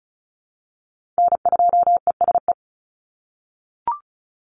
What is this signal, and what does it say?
Morse code, heard as a keyed tone, about 34 wpm, 700 Hz. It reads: N2ESE
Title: NA SOTA Chaser Callsigns 35wpm